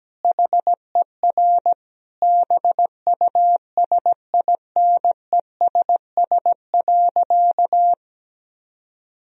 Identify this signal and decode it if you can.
Morse code, heard as a keyed tone, about 17 wpm, 700 Hz. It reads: HER BUSINESS.
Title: Sentences from Top 1000 Words - Review Part 1 - 17wpm